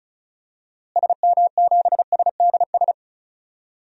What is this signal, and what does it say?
Morse code, heard as a keyed tone, about 36 wpm, 700 Hz. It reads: SM7SDS
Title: DXCC-100 Call Signs - 6 Characters - Repeated 3x - 35wpm